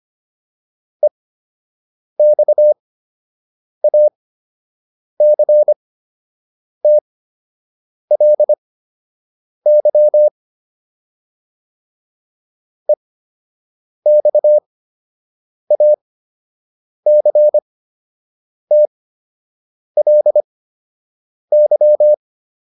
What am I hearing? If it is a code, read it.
Morse code, heard as a keyed tone, about 25 wpm, spaced as for 7 wpm, 600 Hz. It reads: EXACTLY EXACTLY